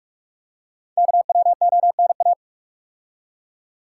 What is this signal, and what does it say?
Morse code, heard as a keyed tone, about 44 wpm, 690 Hz. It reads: KWONA
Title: Code Groups - 5 Letters - 45wpm